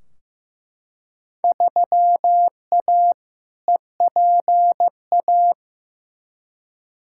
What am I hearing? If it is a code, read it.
Morse code, heard as a keyed tone, about 15 wpm, 700 Hz. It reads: 3A EPA
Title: Field Day Exchanges - 15wpm